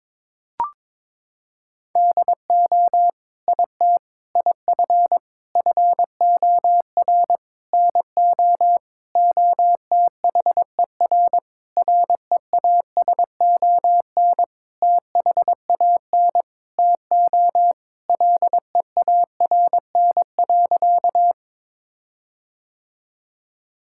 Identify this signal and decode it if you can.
Morse code, heard as a keyed tone, about 22 wpm, 700 Hz. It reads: DO IT IF FOR NO OTHER REASON THAN TO LEARN.
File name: Sentences with 7-Word Phrases - 22wpm